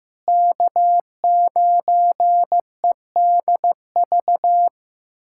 Morse code, heard as a keyed tone, about 16 wpm, 700 Hz. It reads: K9EDV
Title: USA Callsign DE Callsign - 5 Characters - 15wpm